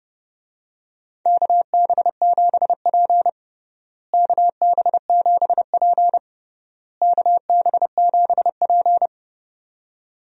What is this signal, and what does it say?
Morse code, heard as a keyed tone, about 30 wpm, 700 Hz. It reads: KB7P KB7P KB7P
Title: USA Call Signs - 4 Characters - Repeated 3x - 30wpm